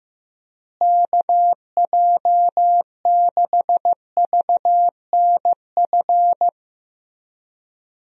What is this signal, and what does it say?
Morse code, heard as a keyed tone, about 15 wpm, 700 Hz. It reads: KJ6VNF
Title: USA Call Signs - 6 Characters - Repeated 3x - 15wpm